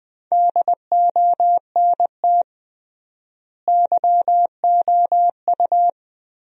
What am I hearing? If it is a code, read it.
Morse code, heard as a keyed tone, about 21 wpm, 700 Hz. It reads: DONT YOU
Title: 3x Word-Spacing - Sentences from Top 300 Words - 20wpm